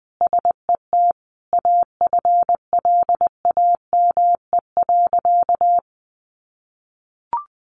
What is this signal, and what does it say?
Morse code, heard as a keyed tone, about 20 wpm, 700 Hz. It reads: SET AFLAME.
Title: A Sound Of Thunder 20wpm